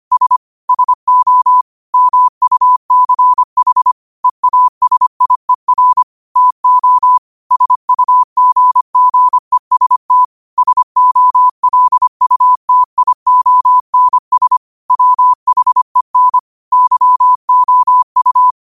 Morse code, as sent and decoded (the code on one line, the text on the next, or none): ... / ... --- / -- ..- -.-. .... / . .- ... .. .-. / - --- / ... ..- --. --. ... - / ... --- .-.. ..- - .. --- -. ... / .-- .... -. / -.-- --- ..-
S SO MUCH EASIER TO SUGGEST SOLUTIONS WHEN YOU